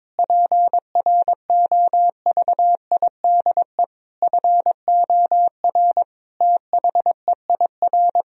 Morse code, heard as a keyed tone, about 22 wpm, 700 Hz. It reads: PROVIDE FOR THEIR
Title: Sentences from Top 1000 Words - Review Part 3 - 22wpm